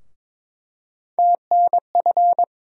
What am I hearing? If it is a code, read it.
Morse code, heard as a keyed tone, about 22 wpm, 700 Hz. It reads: TNF
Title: USA Call Sign Postfixes - 3 Letters 22wpm